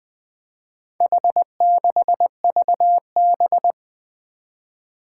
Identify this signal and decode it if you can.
Morse code, heard as a keyed tone, about 21 wpm, 700 Hz. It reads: H6VB
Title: Four Challenging Letter-Number Combinations 20wpm